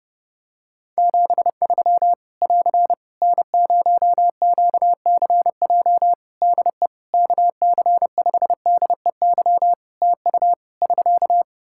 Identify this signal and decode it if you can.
Morse code, heard as a keyed tone, about 30 wpm, 700 Hz. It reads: 73 <AR> N0QCJ DE KC5DEY TU <SK>
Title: Practice QSO - Exchange 4 - Ending - 30wpm